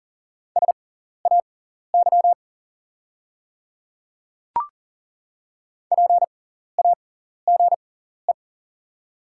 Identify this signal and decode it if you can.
Morse code, heard as a keyed tone, about 40 wpm, spaced as for 14 wpm, 700 Hz. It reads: SAY PAGE
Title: 6x Character Spacing - Top 500 Words - 40wpm